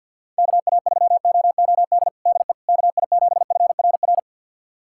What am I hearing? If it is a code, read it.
Morse code, heard as a keyed tone, about 49 wpm, 700 Hz. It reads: KA3OOD DE KI7FRR
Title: USA Callsign DE Callsign 50wpm